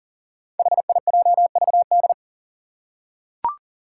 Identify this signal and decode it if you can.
Morse code, heard as a keyed tone, about 38 wpm, 700 Hz. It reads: HIJVD
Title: Five Challenging Letter-Number Combinations 40wpm